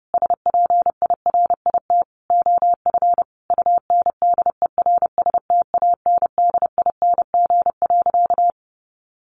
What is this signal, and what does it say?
Morse code, heard as a keyed tone, about 30 wpm, 700 Hz. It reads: SPIRIT OF UNDERSTANDING.
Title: Sentences from Top 1400 Words 30wpm